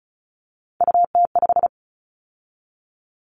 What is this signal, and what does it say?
Morse code, heard as a keyed tone, about 36 wpm, 700 Hz. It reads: UT5
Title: Callsign Prefixes for Top 25 DXCC - 35wpm